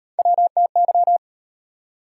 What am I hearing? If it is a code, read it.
Morse code, heard as a keyed tone, about 38 wpm, 700 Hz. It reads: WTY